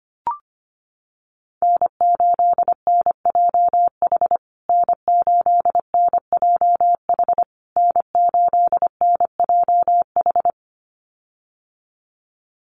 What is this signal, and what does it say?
Morse code, heard as a keyed tone, about 25 wpm, 700 Hz. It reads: N8NJH N8NJH N8NJH
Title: USA Call Signs Repeated 3x 25wpm